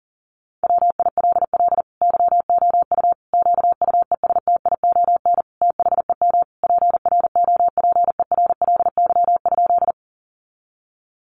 Morse code, heard as a keyed tone, about 40 wpm, 700 Hz. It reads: WILL YOU QUESTION THEM PROPERLY?
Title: Sentences from Top 1100 Words 40wpm